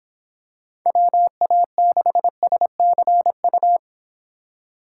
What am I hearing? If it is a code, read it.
Morse code, heard as a keyed tone, about 26 wpm, 700 Hz. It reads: WA6SCU